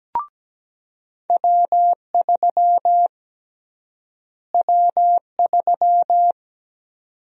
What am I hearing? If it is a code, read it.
Morse code, heard as a keyed tone, about 17 wpm, 700 Hz. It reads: W3 W3
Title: USA Call Sign Prefixes - Repeated 2x - 17wpm